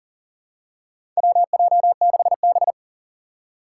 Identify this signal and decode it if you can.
Morse code, heard as a keyed tone, about 40 wpm, 700 Hz. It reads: WJ6B
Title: Four Challenging Letter-Number Combinations 40wpm